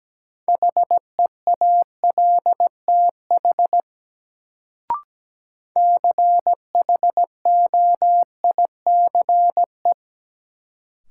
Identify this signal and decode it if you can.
Morse code, heard as a keyed tone, about 17 wpm, 700 Hz. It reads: HEALTH CHOICE
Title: Sets of 5 Words - New Word Drill - 17wpm